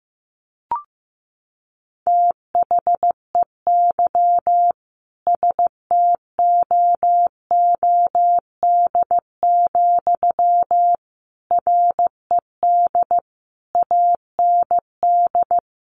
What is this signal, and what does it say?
Morse code, heard as a keyed tone, about 14 wpm, 700 Hz. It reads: THEY STOOD, RED AND
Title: A Sound Of Thunder 15wpm